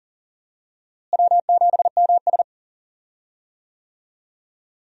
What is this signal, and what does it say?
Morse code, heard as a keyed tone, about 40 wpm, 700 Hz. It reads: W7MS
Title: USA Call Signs - 4 Characters - Repeated 3x - 40wpm